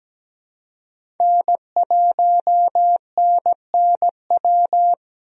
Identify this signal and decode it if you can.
Morse code, heard as a keyed tone, about 17 wpm, 700 Hz. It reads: N1NNW
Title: USA Call Signs - 5 Characters - Repeated 3x - 17wpm